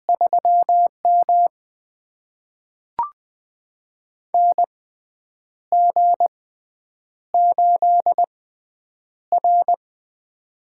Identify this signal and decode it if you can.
Morse code, heard as a keyed tone, about 20 wpm, spaced as for 8 wpm, 700 Hz. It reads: 3M NG8R
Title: USA Call Signs - 6x Character Spacing - 20wpm